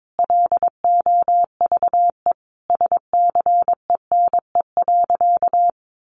Morse code, heard as a keyed tone, about 22 wpm, 700 Hz. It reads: LOVE SCENE.
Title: Sentences from Top 1300 Words 22wpm